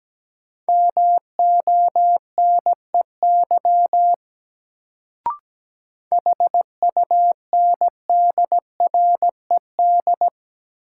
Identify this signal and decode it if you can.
Morse code, heard as a keyed tone, about 17 wpm, 700 Hz. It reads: MONEY HUNDRED
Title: Sentences from Top 400 Words - New Word Drill - 17wpm